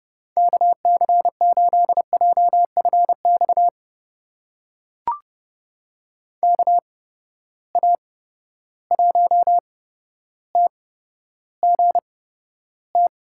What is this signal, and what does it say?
Morse code, heard as a keyed tone, about 32 wpm, spaced as for 13 wpm, 700 Hz. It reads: KC8JFX KA1TGT
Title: USA Call Signs - 6 Characters - 8x Character Spacing - 30wpm